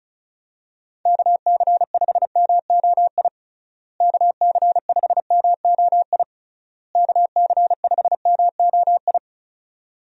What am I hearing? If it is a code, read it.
Morse code, heard as a keyed tone, about 36 wpm, 700 Hz. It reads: KC5MOI KC5MOI KC5MOI
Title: USA Call Signs - 6 Characters - Repeated 3x - 35wpm